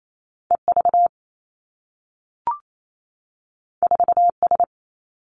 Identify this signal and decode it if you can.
Morse code, heard as a keyed tone, about 28 wpm, 700 Hz. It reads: E V 4 S
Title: Two Challenging Letter-Number Combinations 28wpm